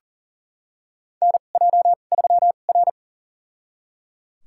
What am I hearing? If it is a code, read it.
Morse code, heard as a keyed tone, about 38 wpm, spaced as for 28 wpm, 700 Hz. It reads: NJ3R
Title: USA Call Signs - 4 Characters - 2x Character Spacing - 40wpm